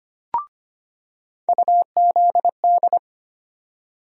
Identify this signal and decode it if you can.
Morse code, heard as a keyed tone, about 25 wpm, 700 Hz. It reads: UZD
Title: USA Call Sign Postfixes - 3 Letters 25wpm